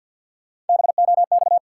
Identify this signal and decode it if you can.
Morse code, heard as a keyed tone, about 51 wpm, 700 Hz. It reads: BOX
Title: Sentences from Top 500 Words - New Word Drill - 50wpm